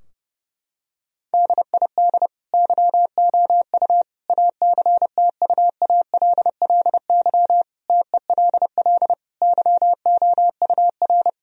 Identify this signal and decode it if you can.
Morse code, heard as a keyed tone, about 30 wpm, 700 Hz. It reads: DID YOU ACTUALLY TELL YOUR
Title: Sentences from Top 1000 Words - Review Part 4 - 30wpm